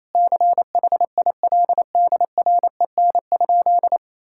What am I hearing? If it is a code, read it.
Morse code, heard as a keyed tone, about 28 wpm, 700 Hz. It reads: CHILDREN?
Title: Sentences from Top 1000 Words - Review Part 5 - 28wpm